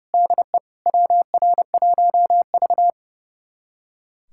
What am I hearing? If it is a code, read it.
Morse code, heard as a keyed tone, about 30 wpm, 700 Hz. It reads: DE WR1V
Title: USA Callsign DE Callsign - 4 Characters - 30wpm